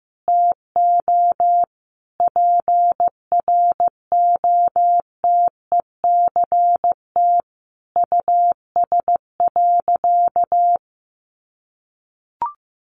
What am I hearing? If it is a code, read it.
Morse code, heard as a keyed tone, about 15 wpm, 700 Hz. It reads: TO PROTECT US.
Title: Sentences from Top 900 Words 15wpm